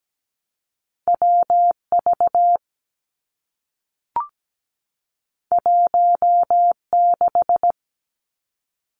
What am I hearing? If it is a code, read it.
Morse code, heard as a keyed tone, about 17 wpm, 700 Hz. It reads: WV 16
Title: Two Challenging Letter-Number Combinations 17wpm